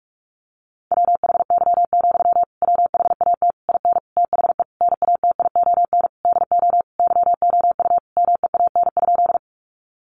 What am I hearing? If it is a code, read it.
Morse code, heard as a keyed tone, about 44 wpm, 700 Hz. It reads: WHY, WHAT IN THE NATION DO YOU MEAN?
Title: Sentences from Top 900 Words 45wpm